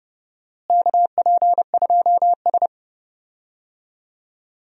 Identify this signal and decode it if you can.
Morse code, heard as a keyed tone, about 32 wpm, 700 Hz. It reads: KP2S
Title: DXCC-100 Call Signs - 4 Characters - Repeated 3x - 30wpm